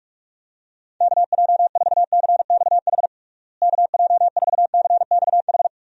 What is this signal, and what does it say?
Morse code, heard as a keyed tone, about 41 wpm, 700 Hz. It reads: KJ4CXH KJ4CXH
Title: USA Call Signs - 6 Characters - Repeated 3x - 45wpm